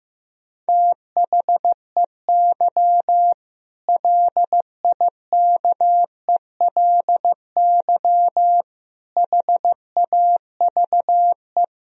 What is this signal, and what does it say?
Morse code, heard as a keyed tone, about 15 wpm, 700 Hz. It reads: THEY LIKELY HAVE
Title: Sentences from Top 1000 Words - Review Part 4 - 15wpm